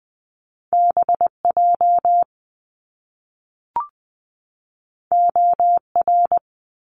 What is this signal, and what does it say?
Morse code, heard as a keyed tone, about 20 wpm, 700 Hz. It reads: B J O R
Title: Top Two Letter Combinations in Words 20wpm